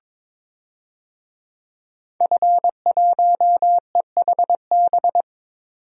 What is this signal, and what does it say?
Morse code, heard as a keyed tone, about 22 wpm, 700 Hz. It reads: F1EHB
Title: DXCC-100 Call Signs - 5 Characters - 22wpm